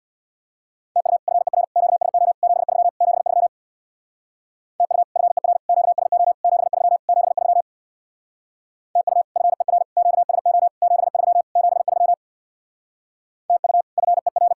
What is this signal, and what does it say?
Morse code, heard as a keyed tone, about 64 wpm, 700 Hz. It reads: TU FER QSO 73 73 TU FER QSO 73 73 TU FER QSO 73 73 TU FER